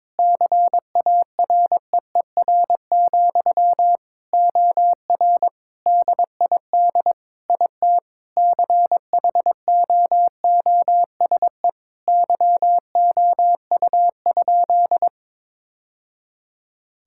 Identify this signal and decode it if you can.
Morse code, heard as a keyed tone, about 22 wpm, 700 Hz. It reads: CAREER, OR DID IT CHOOSE YOU?
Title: Sentences from Top 1000 Words - Review Part 6 - 22wpm